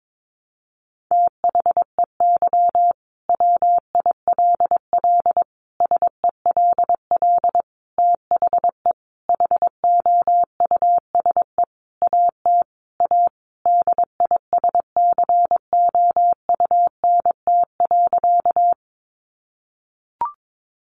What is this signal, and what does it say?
Morse code, heard as a keyed tone, about 22 wpm, 700 Hz. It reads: THEY WILL SELL THE HOUSE AT A DISCOUNT.
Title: Sentences from Top 1200 Words 22wpm